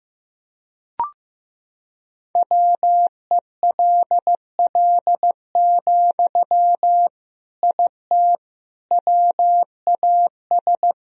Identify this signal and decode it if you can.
Morse code, heard as a keyed tone, about 15 wpm, 700 Hz. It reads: WELL, IT WAS
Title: Sentences from Top 1000 Words - Review Part 3 - 15wpm